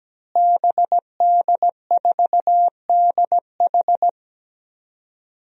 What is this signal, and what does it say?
Morse code, heard as a keyed tone, about 17 wpm, 700 Hz. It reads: BD4DH
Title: DXCC-100 Call Signs - 5 Characters - Repeated 2x - 17wpm